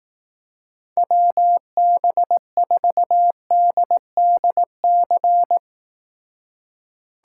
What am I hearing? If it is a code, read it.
Morse code, heard as a keyed tone, about 18 wpm, 700 Hz. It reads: WB4DDC
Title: Speed Racing - USA 2x3 Call Signs 15wpm